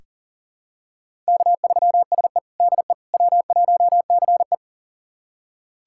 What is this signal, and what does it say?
Morse code, heard as a keyed tone, about 40 wpm, 700 Hz. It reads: K3SE DE W1CE
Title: USA Callsign DE Callsign - 4 Characters - 40wpm